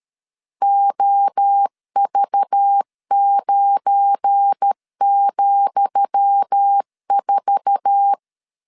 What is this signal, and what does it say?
Morse code, heard as a keyed tone, about 13 wpm, 800 Hz. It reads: OV9,4
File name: LEZIONI / 10 SIGNS / 10D COMA 12wpm 2544